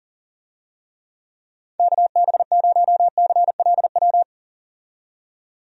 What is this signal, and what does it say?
Morse code, heard as a keyed tone, about 40 wpm, 700 Hz. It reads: KB0CLW
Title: USA Call Signs - 6 Characters - 40wpm